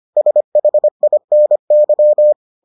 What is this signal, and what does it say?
Morse code, heard as a keyed tone, about 25 wpm, 600 Hz. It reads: SHINY